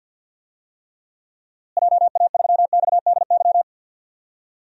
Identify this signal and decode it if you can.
Morse code, heard as a keyed tone, about 47 wpm, 700 Hz. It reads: JA3XDY
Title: DXCC-100 Call Signs - 6 Characters - Repeated 2x - 50wpm